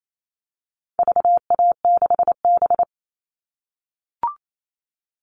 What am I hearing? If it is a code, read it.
Morse code, heard as a keyed tone, about 28 wpm, 700 Hz. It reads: VA6B
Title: All Cut Numbers - 4-Digit 28wpm